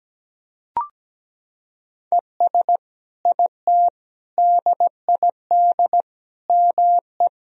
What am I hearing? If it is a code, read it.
Morse code, heard as a keyed tone, about 17 wpm, 700 Hz. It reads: ES IT DID ME
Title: Sentences from Top 200 Words and Abbreviations 17wpm